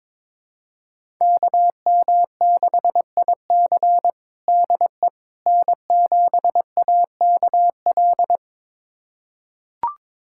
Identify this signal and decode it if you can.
Morse code, heard as a keyed tone, about 21 wpm, 700 Hz. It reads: KM6IC DE N7AKL
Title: USA Callsign DE Callsign - 5 Characters - 22wpm